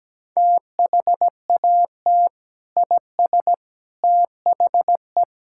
Morse code, heard as a keyed tone, about 17 wpm, 700 Hz. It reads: THAT IS THE
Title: Sentences from Top 1000 Words - Review Part 5 - 17wpm